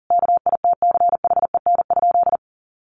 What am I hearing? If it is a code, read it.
Morse code, heard as a keyed tone, about 40 wpm, 700 Hz. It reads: KITCHEN?
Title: Sentences from Top 1400 Words 40wpm